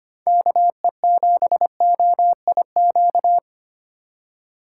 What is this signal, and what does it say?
Morse code, heard as a keyed tone, about 25 wpm, 700 Hz. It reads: KE7OIQ
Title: USA Call Signs - 4x Character Spacing - 25wpm